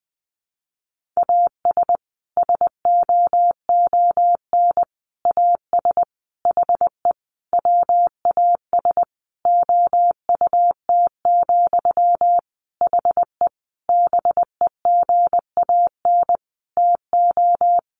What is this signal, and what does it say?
Morse code, heard as a keyed tone, about 20 wpm, 700 Hz. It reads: AS SOON AS HE WAS OUT, HE BEGAN TO
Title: Sentences from Top 1500 Words 20wpm